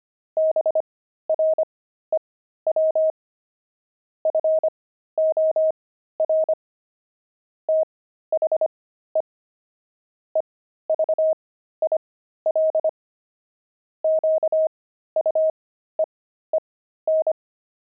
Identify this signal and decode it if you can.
Morse code, heard as a keyed tone, about 25 wpm, spaced as for 13 wpm, 630 Hz. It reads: BREW FOR THE EVIL QUEEN
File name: pangram-fs-13-cs-25-grumpy